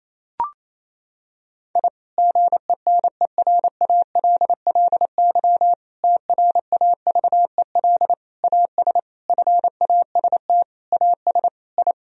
Morse code, heard as a keyed tone, about 27 wpm, 700 Hz. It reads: I GENERALLY TRAVEL AS FAST AS I
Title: Sentences from Top 1000 Words - Review Part 4 - 28wpm